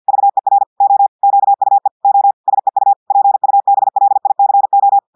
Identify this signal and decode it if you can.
Morse code, heard as a keyed tone, about 50 wpm, 800 Hz. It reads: VER O QUE O SEU PUBLICO
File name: portugais texte en clair 2 vitesse 50 mots minutes